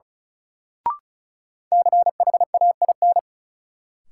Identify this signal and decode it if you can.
Morse code, heard as a keyed tone, about 36 wpm, 700 Hz. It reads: CHAIN
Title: Sentences from Top 1500 Words - New Word Drill Alt - 35wpm